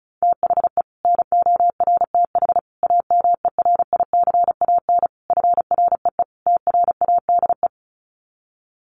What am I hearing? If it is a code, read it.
Morse code, heard as a keyed tone, about 34 wpm, 700 Hz. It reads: THE NORTH AMERICAN FREE TRADE
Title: Sets of 5 Words Encore 35wpm